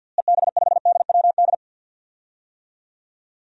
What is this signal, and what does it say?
Morse code, heard as a keyed tone, about 46 wpm, 700 Hz. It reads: EB5DWD
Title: DXCC-100 Call Signs - 6 Characters - Repeated 3x - 50wpm